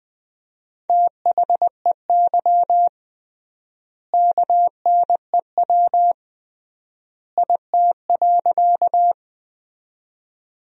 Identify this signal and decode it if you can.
Morse code, heard as a keyed tone, about 20 wpm, 700 Hz. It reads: THEY KNEW IT.
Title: 3x Word-Spacing - Sentences from Top 400 Words - 20wpm